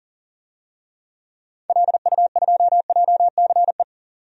Morse code, heard as a keyed tone, about 39 wpm, 700 Hz. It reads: LU2JCE
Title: DXCC-100 Call Signs - 6 Characters - 40wpm